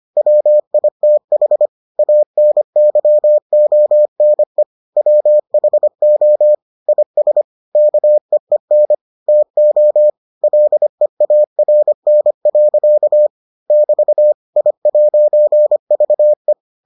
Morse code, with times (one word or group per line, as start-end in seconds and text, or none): 0.2-1.7 WITH
2.0-4.6 ANYONE
5.0-6.5 WHO
6.9-7.4 IS
7.8-8.9 KEEN
9.3-10.1 TO
10.4-13.3 LEARN.
13.7-14.3 <BT>
14.6-16.5 I'VE